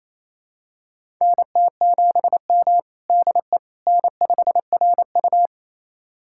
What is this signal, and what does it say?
Morse code, heard as a keyed tone, about 28 wpm, 700 Hz. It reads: NT7M DE N5RU
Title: USA Callsign DE Callsign - 4 Characters - 28wpm